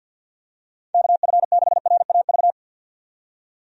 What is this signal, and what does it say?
Morse code, heard as a keyed tone, about 50 wpm, 700 Hz. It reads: KF6RAV
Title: USA Call Signs - 6 Characters - 50wpm